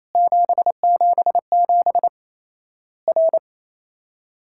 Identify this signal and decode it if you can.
Morse code, heard as a keyed tone, about 28 wpm, 700 Hz. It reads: 777 R
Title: Worked All Europe DX Contest - QTC Exchange - 28wpm